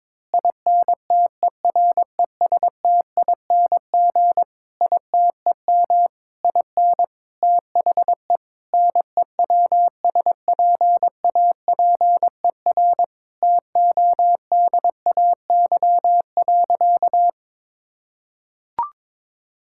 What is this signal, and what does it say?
Morse code, heard as a keyed tone, about 22 wpm, 700 Hz. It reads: INTERESTING ITEM IN THE NEWSPAPER TODAY.
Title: Sentences from Top 1000 Words - Review Part 6 - 22wpm